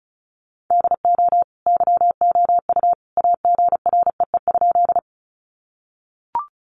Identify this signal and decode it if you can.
Morse code, heard as a keyed tone, about 34 wpm, 700 Hz. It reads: DO YOU AGREE?
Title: Sentences from Top 1100 Words 35wpm